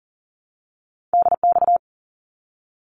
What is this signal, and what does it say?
Morse code, heard as a keyed tone, about 40 wpm, 700 Hz. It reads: DX
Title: ARRL Sections 40wpm